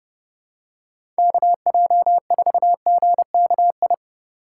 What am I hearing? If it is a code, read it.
Morse code, heard as a keyed tone, about 30 wpm, 700 Hz. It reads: KJ4GKI
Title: USA Call Signs - 6 Characters - 2x Character Spacing - 30wpm